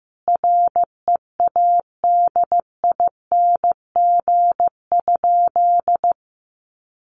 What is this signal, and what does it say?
Morse code, heard as a keyed tone, about 15 wpm, 700 Hz. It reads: READING?
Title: Sentences from Top 1200 Words 15wpm